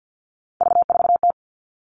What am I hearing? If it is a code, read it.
Morse code, heard as a keyed tone, about 50 wpm, 700 Hz. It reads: V4T
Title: Three Challenging Letter-Number Combinations 50wpm